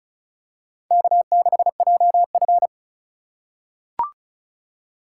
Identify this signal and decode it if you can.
Morse code, heard as a keyed tone, about 35 wpm, 690 Hz. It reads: K6JF
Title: USA Call Signs - 4 Characters - 2x Character Spacing - 35wpm